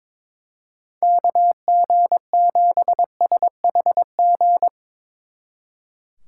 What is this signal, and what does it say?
Morse code, heard as a keyed tone, about 25 wpm, 700 Hz. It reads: KG7SHG
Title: Speed Racing - USA 2x3 Call Signs 22wpm